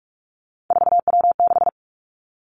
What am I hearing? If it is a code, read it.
Morse code, heard as a keyed tone, about 46 wpm, 700 Hz. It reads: 4W6
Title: Three Challenging Letter-Number Combinations 45wpm